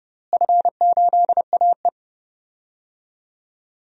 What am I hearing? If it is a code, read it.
Morse code, heard as a keyed tone, about 31 wpm, 700 Hz. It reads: F8AE
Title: DXCC-100 Call Signs - 4 Characters - Repeated 2x - 30wpm